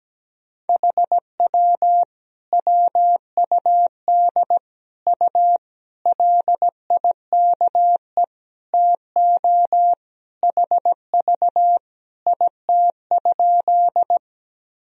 Morse code, heard as a keyed tone, about 17 wpm, 700 Hz. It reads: HW WUD U LIKE TO HV IT?
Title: Sentences from Top 100 Words and Abbreviations 17wpm